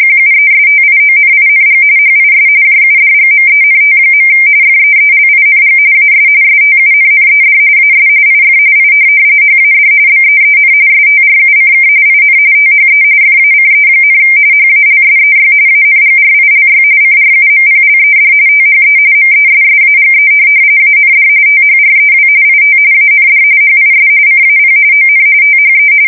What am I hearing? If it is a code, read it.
RTTY, about 45.45 baud. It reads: QUESTIONING HIS JUDGMENT. FEELING AS STUNNED AS IF I HAD BEEN STRUCK IN THE FACE, HALSEY ANGRILY CRUMPLED THE MESSAGE AND THREW IT AND HIS CAP ON THE
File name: fsk170 / USS-New-Jersey